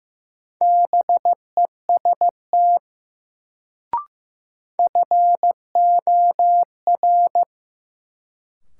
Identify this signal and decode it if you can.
Morse code, heard as a keyed tone, about 15 wpm, 700 Hz. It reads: BEST FOR
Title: Sets of 2 Words - New Word Drill - 15wpm